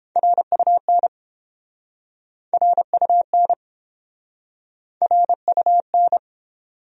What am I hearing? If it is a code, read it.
Morse code, heard as a keyed tone, about 29 wpm, 700 Hz. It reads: RUN RUN RUN